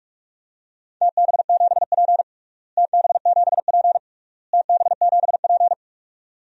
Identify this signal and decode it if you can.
Morse code, heard as a keyed tone, about 47 wpm, 700 Hz. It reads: TB7P TB7P TB7P
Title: DXCC-100 Call Signs - 4 Characters - Repeated 3x - 45wpm